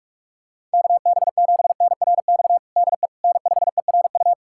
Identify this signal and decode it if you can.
Morse code, heard as a keyed tone, about 46 wpm, 700 Hz. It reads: KB7NRX DE N5ERU
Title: USA Callsign DE Callsign 45wpm